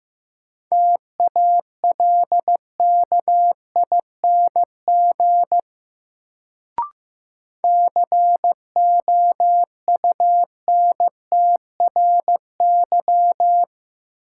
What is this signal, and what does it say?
Morse code, heard as a keyed tone, about 15 wpm, 700 Hz. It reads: TALKING COUNTRY
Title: Sets of 2 Words - New Word Drill - 15wpm